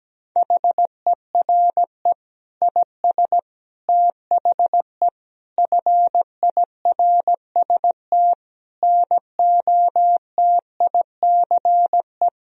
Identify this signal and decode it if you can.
Morse code, heard as a keyed tone, about 17 wpm, 700 Hz. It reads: HERE IS THE FIRST NOTICE
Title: Sentences from Top 1000 Words - Review Part 2 - 17wpm